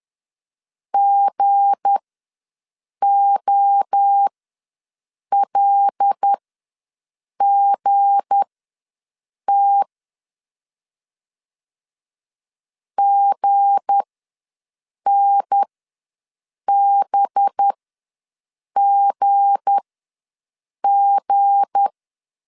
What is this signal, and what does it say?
Morse code, heard as a keyed tone, about 10 wpm, spaced as for 6 wpm, 800 Hz. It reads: GOLGT GNBGG
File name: LEZIONI / 7 GF9Z / 7 G ws 10wpm 3843